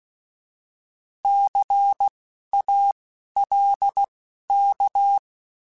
Morse code, heard as a keyed tone, about 16 wpm, spaced as for 12 wpm, 800 Hz. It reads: CALK
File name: Alphabetical / Alphabetical mp3s / Cs @10